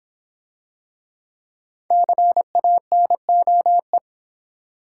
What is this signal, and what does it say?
Morse code, heard as a keyed tone, about 25 wpm, 700 Hz. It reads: CANOE
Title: Easy Words - Speed-Racing with Hints - 20wpm